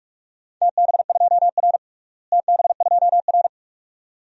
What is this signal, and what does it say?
Morse code, heard as a keyed tone, about 46 wpm, 700 Hz. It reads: TB2R TB2R
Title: DXCC-100 Call Signs - 4 Characters - Repeated 2x - 45wpm